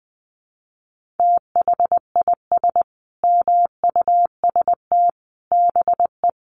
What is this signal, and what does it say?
Morse code, heard as a keyed tone, about 20 wpm, 700 Hz. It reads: THIS MUST BE
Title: Sentences from Top 1100 Words 20wpm